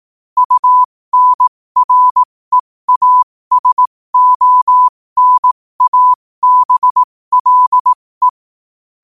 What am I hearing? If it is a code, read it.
Morse code, heard as a keyed tone, about 18 wpm, spaced as for 15 wpm, 1000 Hz. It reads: UNREASONABLE
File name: qotdfiles / 15wpm